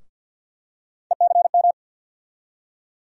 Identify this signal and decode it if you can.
Morse code, heard as a keyed tone, about 47 wpm, 700 Hz. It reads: ECM
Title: USA Call Sign Postfixes - 3 Letters 50wpm